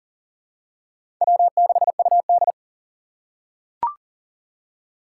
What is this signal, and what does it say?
Morse code, heard as a keyed tone, about 40 wpm, 700 Hz. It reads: W6UD
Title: USA Call Signs - 4 Characters - 3x Character Spacing - 40wpm